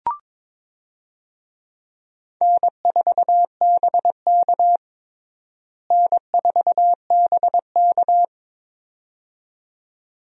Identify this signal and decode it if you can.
Morse code, heard as a keyed tone, about 22 wpm, 700 Hz. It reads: N4BK N4BK
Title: USA Call Signs - 4 Characters - Repeated 2x - 22wpm